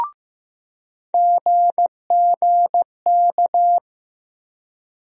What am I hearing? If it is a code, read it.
Morse code, heard as a keyed tone, about 15 wpm, 700 Hz. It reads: GGK
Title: USA Call Sign Postfixes - 3 Letters 15wpm